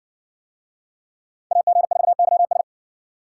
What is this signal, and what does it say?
Morse code, heard as a keyed tone, about 60 wpm, 700 Hz. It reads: AG4YS